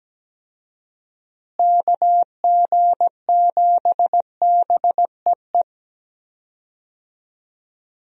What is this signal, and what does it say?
Morse code, heard as a keyed tone, about 17 wpm, 700 Hz. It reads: KG7BEE
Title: USA Call Signs - 6 Characters - Repeated 3x - 17wpm